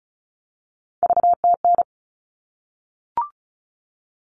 Morse code, heard as a keyed tone, about 34 wpm, 700 Hz. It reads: VTN
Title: All Cut Numbers - 3-Digit 35wpm